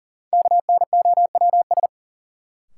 Morse code, heard as a keyed tone, about 40 wpm, 700 Hz. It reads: KNOWS
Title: Sentences from Top 1000 Words - Review Part 1 - New Word Drill - 40wpm